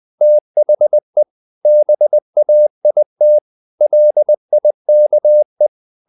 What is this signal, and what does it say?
Morse code, heard as a keyed tone, about 20 wpm, 600 Hz. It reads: THE BAIT LIKE
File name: the old man and the sea / the old man and the sea part 1 20wpm